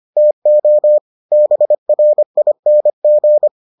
Morse code, heard as a keyed tone, about 25 wpm, 600 Hz. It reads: TO BRING